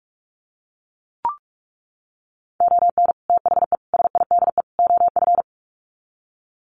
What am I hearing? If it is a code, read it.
Morse code, heard as a keyed tone, about 44 wpm, 700 Hz. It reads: ON THE SIDE OF
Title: Sets of 4 Words 45wpm